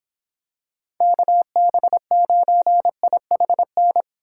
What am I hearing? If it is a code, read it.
Morse code, heard as a keyed tone, about 26 wpm, 700 Hz. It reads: KB9IHN